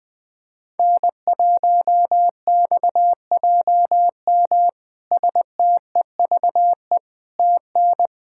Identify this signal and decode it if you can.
Morse code, heard as a keyed tone, about 20 wpm, 700 Hz. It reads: N1XJM STEVE TN
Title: K1USN SST QSO Practice - 20wpm